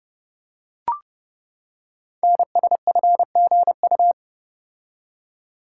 Morse code, heard as a keyed tone, about 30 wpm, 700 Hz. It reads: NSFGU
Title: Code Groups - 5 Letters - 30wpm